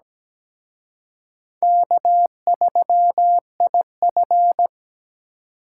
Morse code, heard as a keyed tone, about 20 wpm, 700 Hz. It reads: K3IF
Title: Speed Racing - USA 1x2 Call Signs 17wpm